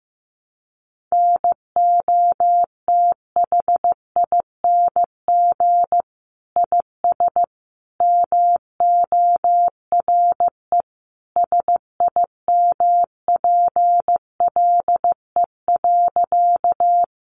Morse code, heard as a keyed tone, about 15 wpm, 700 Hz. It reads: NOTHING IS MORE SIMPLE.
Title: Sentences from Top 500 Words 15wpm